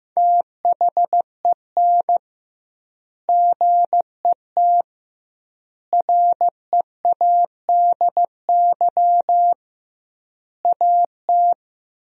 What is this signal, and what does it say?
Morse code, heard as a keyed tone, about 15 wpm, 700 Hz. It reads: THEN GET READY AT
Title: 2x Word-Spacing - Sentences from Top 400 Words - 15wpm